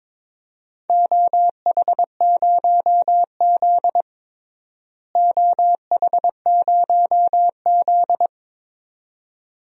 Morse code, heard as a keyed tone, about 22 wpm, 700 Hz. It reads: OH0Z OH0Z
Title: DXCC-100 Call Signs - 4 Characters - Repeated 2x - 22wpm